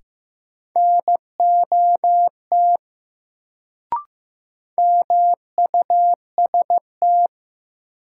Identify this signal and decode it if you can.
Morse code, heard as a keyed tone, about 15 wpm, 700 Hz. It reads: NOT MUST
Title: Sets of 2 Words - New Word Drill - 15wpm